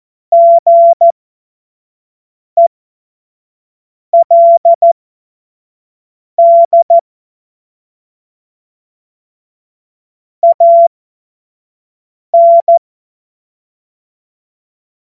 Morse code, heard as a keyed tone, about 14 wpm, spaced as for 5 wpm, 680 Hz. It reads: GELD AN